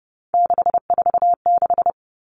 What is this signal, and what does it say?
Morse code, heard as a keyed tone, about 30 wpm, 700 Hz. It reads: 646
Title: Numbers - 3-Digit 30wpm